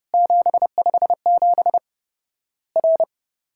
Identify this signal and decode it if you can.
Morse code, heard as a keyed tone, about 30 wpm, 700 Hz. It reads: 757 R
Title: Worked All Europe DX Contest - QTC Exchange - 30wpm